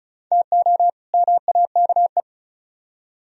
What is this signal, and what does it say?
Morse code, heard as a keyed tone, about 48 wpm, spaced as for 42 wpm, 700 Hz. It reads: TO MAKE
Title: Speed-Racing - Sets of 2 Words 35wpm